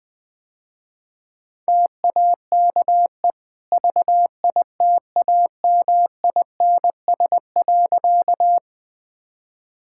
Morse code, heard as a keyed tone, about 20 wpm, 700 Hz. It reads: TAKE VITAMINS.
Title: Easy Sentences with Hints - 20wpm